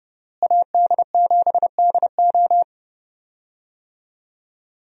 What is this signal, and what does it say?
Morse code, heard as a keyed tone, about 30 wpm, 700 Hz. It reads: AD7DO
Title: USA Call Signs - 5 Characters - Repeated 3x - 30wpm